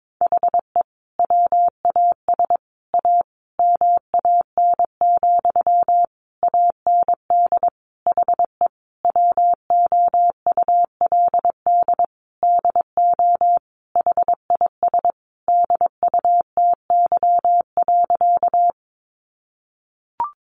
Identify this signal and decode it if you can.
Morse code, heard as a keyed tone, about 22 wpm, 700 Hz. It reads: HE WAS A MAN, AND HE WOULD DO HIS DUTY.
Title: Sentences from Top 1500 Words 22wpm